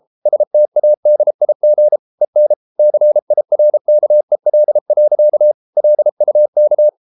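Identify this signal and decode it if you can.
Morse code, heard as a keyed tone, about 33 wpm, 600 Hz. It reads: STADIG EN CIRKEL. LUK
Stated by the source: Morse code at 33 wpm, QRQ.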